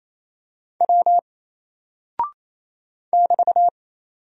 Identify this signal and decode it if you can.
Morse code, heard as a keyed tone, about 28 wpm, 700 Hz. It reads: W <BT>
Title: Single Character (Letter:Number:Punctuation:Prosign) 28wpm